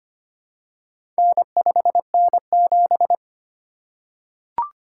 Morse code, heard as a keyed tone, about 25 wpm, 700 Hz. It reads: N5N7
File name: practice / 25wpm / Four Letter-Number Combinations - 25wpm